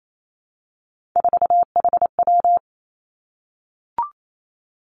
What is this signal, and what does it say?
Morse code, heard as a keyed tone, about 28 wpm, 700 Hz. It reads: 4HW
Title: Three Challenging Letter-Number Combinations 28wpm